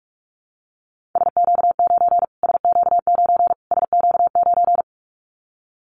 Morse code, heard as a keyed tone, about 44 wpm, 700 Hz. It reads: SQ9 SQ9 SQ9
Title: Callsign Prefixes for Top 25 DXCC - Repeated 3x - 45wpm